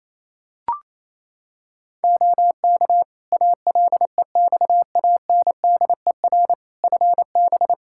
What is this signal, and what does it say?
Morse code, heard as a keyed tone, about 28 wpm, 700 Hz. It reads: OK ALEXANDER FB
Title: K1USN SST QSO Practice 28wpm